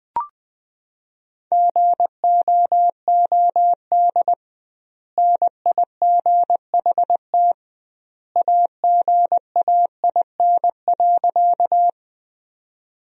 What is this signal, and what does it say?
Morse code, heard as a keyed tone, about 19 wpm, 700 Hz. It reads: GOOD NIGHT AGAIN.
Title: 2x Word-Spacing - Sentences from Top 300 Words - 20wpm